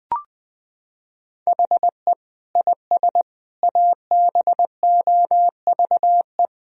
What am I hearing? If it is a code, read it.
Morse code, heard as a keyed tone, about 21 wpm, 700 Hz. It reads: HE IS ABOVE
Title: Sentences from Top 1000 Words - Review Part 3 - 20wpm